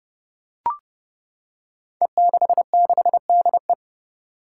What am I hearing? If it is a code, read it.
Morse code, heard as a keyed tone, about 32 wpm, 700 Hz. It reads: E66DE
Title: Five Challenging Letter-Number Combinations 30wpm